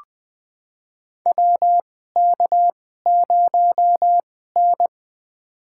Morse code, heard as a keyed tone, about 20 wpm, spaced as for 15 wpm, 700 Hz. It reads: WK0N
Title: USA Call Signs - 4 Characters - 2x Character Spacing - 20wpm